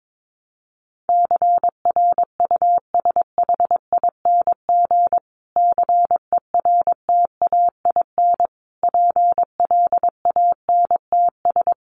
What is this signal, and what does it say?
Morse code, heard as a keyed tone, about 22 wpm, 700 Hz. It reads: CRUSHING CERTAIN PLANTS
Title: A Sound Of Thunder 22wpm